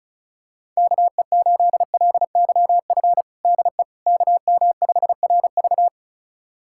Morse code, heard as a keyed tone, about 36 wpm, 700 Hz. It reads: KE8LYF DE KM5RV
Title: USA Callsign DE Callsign 35wpm